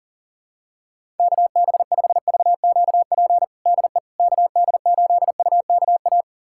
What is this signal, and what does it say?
Morse code, heard as a keyed tone, about 40 wpm, 700 Hz. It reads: KB5VQP DE KD8UKA
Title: USA Callsign DE Callsign - 6 Characters - 40wpm